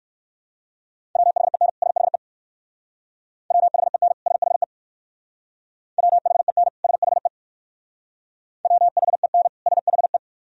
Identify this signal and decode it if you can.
Morse code, heard as a keyed tone, about 57 wpm, 700 Hz. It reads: WHEN SHE WHEN SHE WHEN SHE WHEN SHE